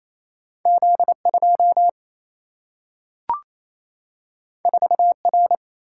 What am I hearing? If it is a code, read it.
Morse code, heard as a keyed tone, about 28 wpm, 700 Hz. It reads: Z2 4R
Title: Single-Letter Single-Number 28wpm